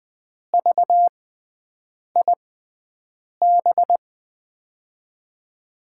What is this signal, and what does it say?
Morse code, heard as a keyed tone, about 20 wpm, spaced as for 7 wpm, 700 Hz. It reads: VIB
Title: USA Call Signs - 6x Character Spacing - 20wpm